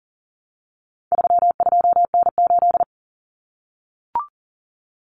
Morse code, heard as a keyed tone, about 40 wpm, 700 Hz. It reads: 32N8
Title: Common Cut Numbers - 4-Digit 40wpm